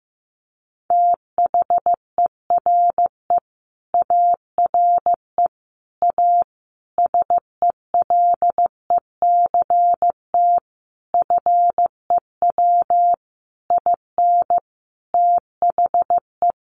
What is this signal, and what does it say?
Morse code, heard as a keyed tone, about 15 wpm, 700 Hz. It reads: THERE ARE A SELECT FEW IN THE
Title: Sentences from Top 1300 Words 15wpm